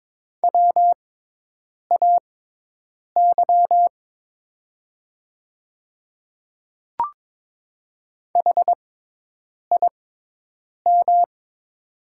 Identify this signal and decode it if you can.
Morse code, heard as a keyed tone, about 23 wpm, spaced as for 8 wpm, 700 Hz. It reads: WAY HIM
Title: 6x Character Spacing - Top 200 Words - 22wpm